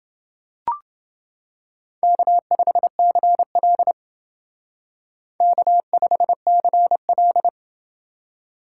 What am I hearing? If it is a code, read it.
Morse code, heard as a keyed tone, about 28 wpm, 700 Hz. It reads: K5CL K5CL